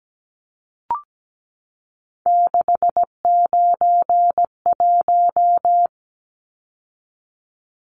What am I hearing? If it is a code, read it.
Morse code, heard as a keyed tone, about 17 wpm, 700 Hz. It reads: 691
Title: Numbers - 3-Digit 17wpm